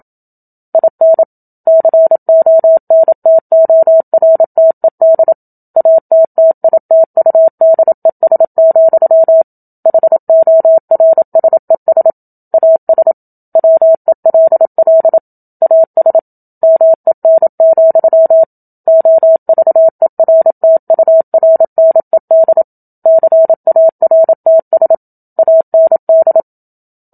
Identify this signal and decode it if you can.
Morse code, heard as a keyed tone, about 27 wpm, 650 Hz. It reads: IN CONTORTED ATTITUDES, HORSES AS WELL AS MEN, OVERTURNED CARTS AND
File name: Book Two / 01-Chapter 1 - 27 WPM